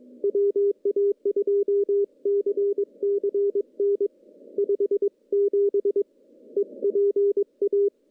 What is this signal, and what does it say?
Morse code, heard as a keyed tone, about 22 wpm, 410 Hz. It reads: WA2CCN 57 EPA
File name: contests / CW-SS-k5tr- -0200